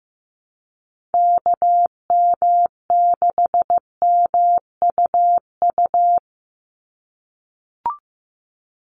Morse code, heard as a keyed tone, about 15 wpm, 700 Hz. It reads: KM6MUU
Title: USA Call Signs 15wpm